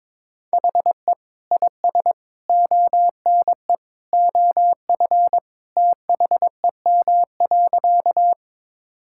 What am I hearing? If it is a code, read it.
Morse code, heard as a keyed tone, about 22 wpm, 700 Hz. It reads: HE IS ONE OF THEM.
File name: practice / 22wpm / Sentences with 3-Word Phrases - 22wpm